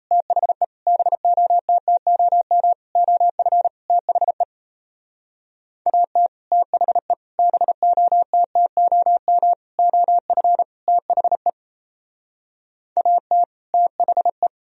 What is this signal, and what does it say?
Morse code, heard as a keyed tone, about 33 wpm, 700 Hz. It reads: THE BOTTOM OF THE AT THE BOTTOM OF THE AT THE
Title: Speed-Racing - Sets of 5 Words 28wpm